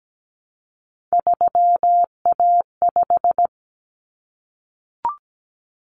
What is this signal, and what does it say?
Morse code, heard as a keyed tone, about 17 wpm, 700 Hz. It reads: 3A5
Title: Common Cut Numbers - 3-Digit 17wpm